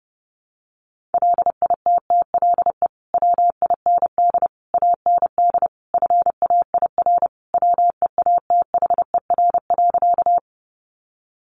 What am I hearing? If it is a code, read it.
Morse code, heard as a keyed tone, about 30 wpm, 700 Hz. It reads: LITTLE WIND AND FAIR WEATHER.
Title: Sentences from Top 1300 Words 30wpm